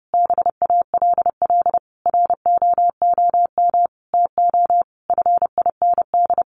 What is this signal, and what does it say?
Morse code, heard as a keyed tone, about 30 wpm, 700 Hz. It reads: BALL ROOM TO FIND
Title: Sentences from Top 1300 Words 30wpm